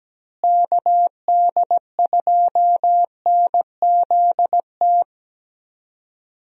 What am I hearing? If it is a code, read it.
Morse code, heard as a keyed tone, about 17 wpm, 700 Hz. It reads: KD2NZT
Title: USA Call Signs - 6 Characters - Repeated 2x - 17wpm